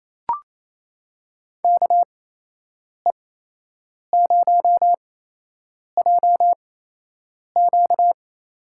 Morse code, heard as a keyed tone, about 28 wpm, spaced as for 8 wpm, 700 Hz. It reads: KE0JQ
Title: USA Call Signs - 6 Characters - 8x Character Spacing - 28wpm